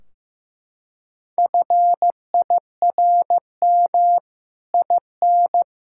Morse code, heard as a keyed tone, about 15 wpm, 700 Hz. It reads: FIRM IN
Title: Sentences from Top 1600 Words 15wpm